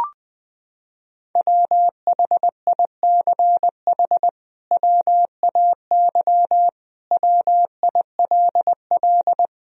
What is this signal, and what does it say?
Morse code, heard as a keyed tone, about 21 wpm, 700 Hz. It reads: WHICH WAY WILL
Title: Sentences from Top 1000 Words - Review Part 2 - 20wpm